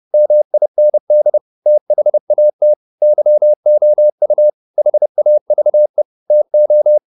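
Morse code, heard as a keyed tone, about 30 wpm, 600 Hz. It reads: MIND THAT YOU HAVE TO
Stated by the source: Morse code at 30 wpm, the novel Never by Ken Follett in English, from CW med Gnister.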